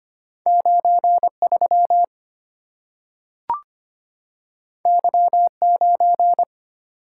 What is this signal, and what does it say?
Morse code, heard as a keyed tone, about 25 wpm, 700 Hz. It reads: 93 Y9
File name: Two Letter-Number Combinations - 25wpm